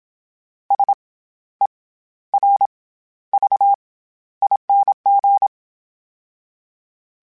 Morse code, heard as a keyed tone, about 27 wpm, 800 Hz. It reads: S E R V ING
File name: Special INGs-mp3s / Special INGs-mp3s / SlowToFaster ing @25